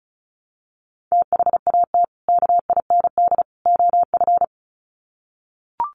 Morse code, heard as a keyed tone, about 34 wpm, 700 Hz. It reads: THAT KIND OF
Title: Sets of 3 Words 35wpm